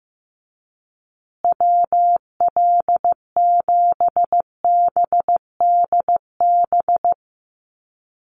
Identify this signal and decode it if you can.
Morse code, heard as a keyed tone, about 15 wpm, 700 Hz. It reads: WL7BDB
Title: USA Call Signs Repeated 2x 15wpm